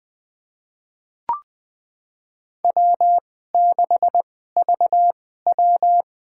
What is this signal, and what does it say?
Morse code, heard as a keyed tone, about 21 wpm, spaced as for 15 wpm, 700 Hz. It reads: W6VW
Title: USA Call Signs - 4 Characters - 2x Character Spacing - 20wpm